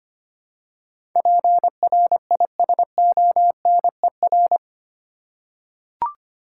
Morse code, heard as a keyed tone, about 25 wpm, 700 Hz. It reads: PRISONER
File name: Eight Letter Words - 25wpm